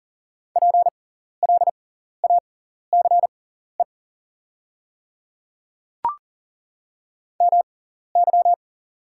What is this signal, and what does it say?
Morse code, heard as a keyed tone, about 40 wpm, spaced as for 14 wpm, 690 Hz. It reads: PLACE MY
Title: 6x Character Spacing - Top 500 Words - 40wpm